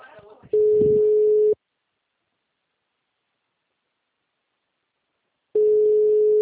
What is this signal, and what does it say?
Ringback tone (Europe, 425 Hz), with speech.